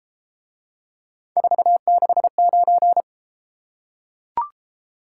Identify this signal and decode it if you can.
Morse code, heard as a keyed tone, about 31 wpm, 700 Hz. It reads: 469